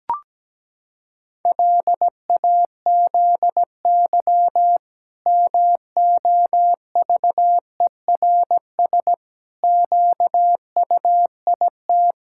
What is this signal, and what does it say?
Morse code, heard as a keyed tone, about 17 wpm, 700 Hz. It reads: LAZY MOVERS QUIT
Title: Pangram Challenge v2 - 17wpm